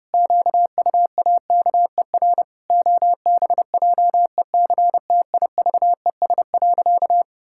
Morse code, heard as a keyed tone, about 30 wpm, 700 Hz. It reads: QUAKER OBJECTIVES.
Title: Pangram Challenge v2 - 30wpm